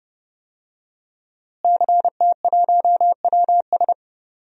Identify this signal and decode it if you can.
Morse code, heard as a keyed tone, about 31 wpm, 700 Hz. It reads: CT1WS
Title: DXCC-100 Call Signs - 5 Characters - Repeated 2x - 30wpm